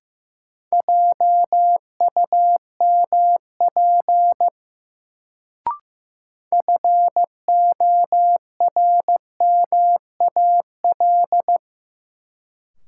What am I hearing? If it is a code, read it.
Morse code, heard as a keyed tone, about 15 wpm, 700 Hz. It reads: JUMP FORMAL
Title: Sentences from Top 1300 Words - New Word Drill Alt - 15wpm